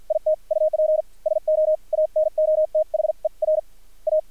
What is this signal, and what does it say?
Morse code, heard as a keyed tone, about 55 wpm, spaced as for 45 wpm, 620 Hz, with radio noise. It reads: IT UP SO ANOTHER A
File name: archived / CRSnet